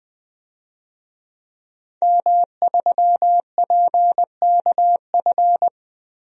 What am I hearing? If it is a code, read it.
Morse code, heard as a keyed tone, about 20 wpm, 700 Hz. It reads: M3PKF
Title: DXCC-100 Call Signs - 5 Characters - Repeated 3x - 20wpm